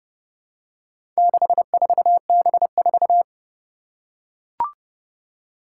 Morse code, heard as a keyed tone, about 30 wpm, 700 Hz. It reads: 64B4
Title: Four Challenging Letter-Number Combinations 30wpm